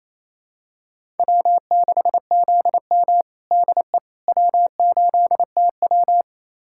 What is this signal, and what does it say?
Morse code, heard as a keyed tone, about 28 wpm, 700 Hz. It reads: W6ZM DE W8TW
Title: USA Callsign DE Callsign - 4 Characters - 28wpm